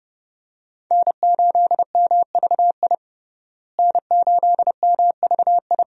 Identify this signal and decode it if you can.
Morse code, heard as a keyed tone, about 30 wpm, 700 Hz. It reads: N8MVI N8MVI
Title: USA Call Signs - 5 Characters - Repeated 2x - 30wpm